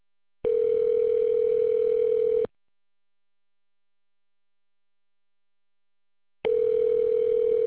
Ringback tone (North America), 440 + 480 Hz.